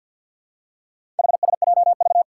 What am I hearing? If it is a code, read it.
Morse code, heard as a keyed tone, about 49 wpm, 700 Hz. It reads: HSJV
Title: Four Challenging Letter-Number Combinations 50wpm